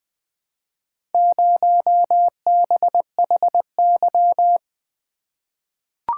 Morse code, heard as a keyed tone, about 20 wpm, 700 Hz. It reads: 0BHY
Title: Four Letter-Number Combinations - 20wpm